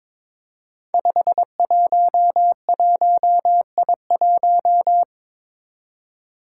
Morse code, heard as a keyed tone, about 22 wpm, 700 Hz. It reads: 511I1
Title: Five Challenging Letter-Number Combinations 22wpm